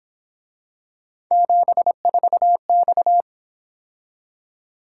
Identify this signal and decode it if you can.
Morse code, heard as a keyed tone, about 27 wpm, 700 Hz. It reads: 74X
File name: Speed-Racing - Three Letter-Number Combinations - 22wpm